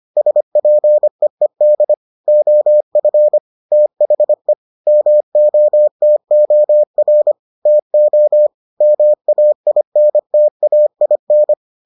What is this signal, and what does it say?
Morse code, heard as a keyed tone, about 25 wpm, 600 Hz. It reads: SPEED OF THE MOTOR TO MAINTAIN